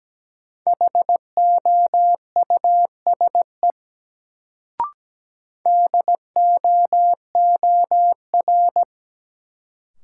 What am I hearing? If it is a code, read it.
Morse code, heard as a keyed tone, about 17 wpm, 700 Hz. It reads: HOUSE DOOR
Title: Sets of 2 Words - New Word Drill - 17wpm